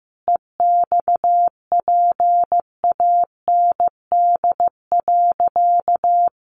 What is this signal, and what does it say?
Morse code, heard as a keyed tone, about 15 wpm, 700 Hz. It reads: EXPAND.
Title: Sentences from Top 1400 Words 15wpm